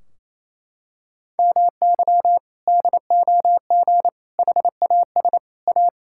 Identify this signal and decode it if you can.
Morse code, heard as a keyed tone, about 28 wpm, 700 Hz. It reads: MY DOG HAS A
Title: Sentences from Top 1600 Words 28wpm